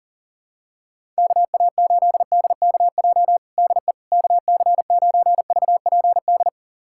Morse code, heard as a keyed tone, about 37 wpm, 700 Hz. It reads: KA8DKJ DE KC9VPD
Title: USA Callsign DE Callsign - 6 Characters - 40wpm